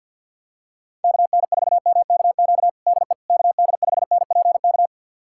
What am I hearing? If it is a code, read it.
Morse code, heard as a keyed tone, about 48 wpm, 700 Hz. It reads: KN4MKQ DE KD5NPK
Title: USA Callsign DE Callsign - 6 Characters - 50wpm